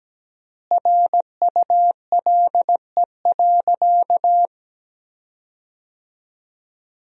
Morse code, heard as a keyed tone, about 17 wpm, 700 Hz. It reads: RULE.
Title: 3x Word-Spacing - Sentences from Top 400 Words - 17wpm